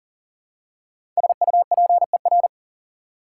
Morse code, heard as a keyed tone, about 40 wpm, 700 Hz. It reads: SUPER
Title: Prefixes and Suffixes 40wpm